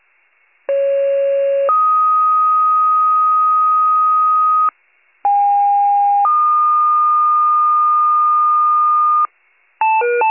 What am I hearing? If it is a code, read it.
Two-tone page, heard as a one-second tone, then a three-second tone, about 550 then 1250 Hz, about 810 then 1220 Hz. It is heard from 0.7 to 9.3 s.